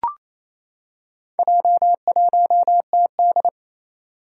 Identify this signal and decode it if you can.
Morse code, heard as a keyed tone, about 28 wpm, 700 Hz. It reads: J1TD
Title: Four Challenging Letter-Number Combinations 28wpm